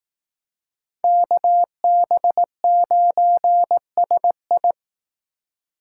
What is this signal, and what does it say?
Morse code, heard as a keyed tone, about 18 wpm, 700 Hz. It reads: KB9SI
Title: Speed Racing - USA 2x2 Call Signs 15wpm